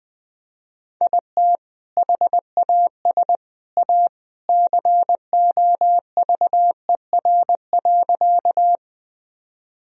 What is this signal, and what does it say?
Morse code, heard as a keyed tone, about 20 wpm, 700 Hz. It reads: IT HAS A COVER.
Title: Sentences from Top 1000 Words - Review Part 1 - 20wpm